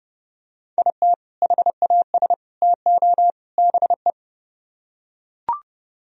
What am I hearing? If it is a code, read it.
Morse code, heard as a keyed tone, about 30 wpm, 700 Hz. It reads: IT HAS TO BE